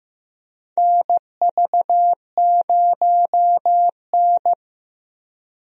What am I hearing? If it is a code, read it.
Morse code, heard as a keyed tone, about 15 wpm, 700 Hz. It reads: NV0N
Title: USA Call Signs - 4 Characters - Repeated 3x - 15wpm